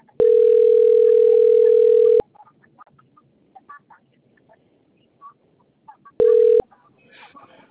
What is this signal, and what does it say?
Ringback tone (North America), 440 + 480 Hz.
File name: rg-1600-unknown- - - .82198